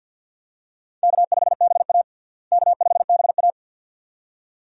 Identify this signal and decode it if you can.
Morse code, heard as a keyed tone, about 52 wpm, 700 Hz. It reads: K5BA K5BA
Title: USA Call Signs - 4 Characters - Repeated 2x - 50wpm